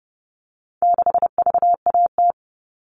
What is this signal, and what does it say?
Morse code, heard as a keyed tone, about 30 wpm, 700 Hz. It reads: 6VAT
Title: All Cut Numbers - 4-Digit 30wpm